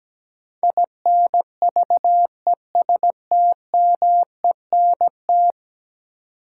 Morse code, heard as a keyed tone, about 17 wpm, 700 Hz. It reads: INVESTMENT
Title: Sentences from Top 700 Words - New Word Drill - 17wpm